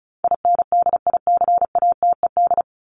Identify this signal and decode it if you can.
Morse code, heard as a keyed tone, about 34 wpm, 700 Hz. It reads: INDICATED
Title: ED Suffix - Top 100 Words with Base Words 35wpm